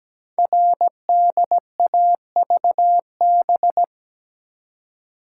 Morse code, heard as a keyed tone, about 17 wpm, 700 Hz. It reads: RDAVB
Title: Code Groups - 5 Letters - 17wpm